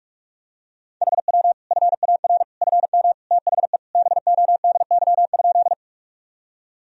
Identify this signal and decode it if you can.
Morse code, heard as a keyed tone, about 46 wpm, 700 Hz. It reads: HW FAR FM THE BODY?
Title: Sentences from Top 400 Words and Abbreviations 45wpm